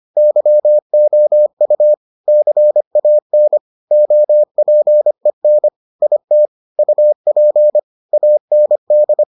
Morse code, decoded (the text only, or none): YOU CAN OPEN IT UP AND